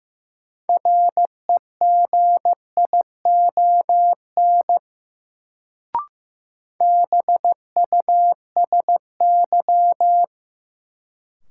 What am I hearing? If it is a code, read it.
Morse code, heard as a keyed tone, about 15 wpm, 700 Hz. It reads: REGION BUSY
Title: Sentences from Top 1500 Words - New Word Drill Alt - 15wpm